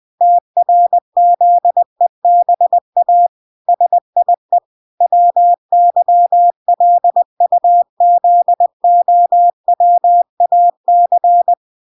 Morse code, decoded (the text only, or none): TRZEBA SIE WYLUZOWAC